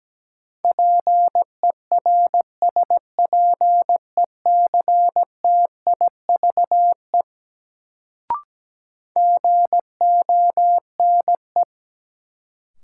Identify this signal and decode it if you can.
Morse code, heard as a keyed tone, about 17 wpm, 700 Hz. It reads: PERSPECTIVE GONE
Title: Sentences from Top 1200 Words - New Word Drill - 17wpm